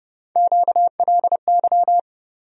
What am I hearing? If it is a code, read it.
Morse code, heard as a keyed tone, about 31 wpm, 700 Hz. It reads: QLY
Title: USA Call Sign Postfixes - 3 Letters 30wpm